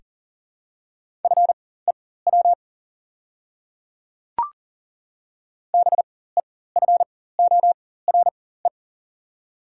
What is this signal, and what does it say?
Morse code, heard as a keyed tone, about 40 wpm, spaced as for 19 wpm, 690 Hz. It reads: FEW BEFORE
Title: 4x Character Spacing - Top 300 Words - 40wpm